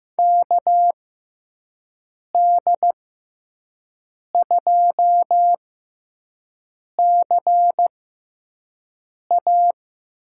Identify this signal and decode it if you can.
Morse code, heard as a keyed tone, about 15 wpm, spaced as for 5 wpm, 700 Hz. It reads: KD2CA
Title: USA Call Signs - 6 Characters - 6x Character Spacing - 15wpm